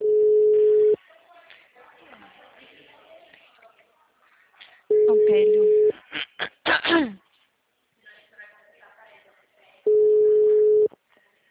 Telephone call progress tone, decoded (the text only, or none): ringback tone